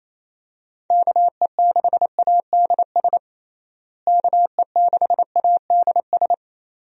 Morse code, heard as a keyed tone, about 28 wpm, 700 Hz. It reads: KE6ADS KE6ADS
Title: USA Call Signs - 6 Characters - Repeated 2x - 28wpm